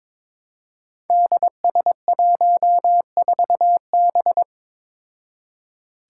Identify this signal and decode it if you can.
Morse code, heard as a keyed tone, about 22 wpm, 700 Hz. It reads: DS14B
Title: Five Challenging Letter-Number Combinations 22wpm